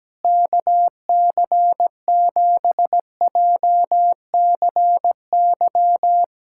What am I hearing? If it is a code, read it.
Morse code, heard as a keyed tone, about 17 wpm, 700 Hz. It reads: KC7JCY
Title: USA Callsign DE Callsign - 6 Characters - 17wpm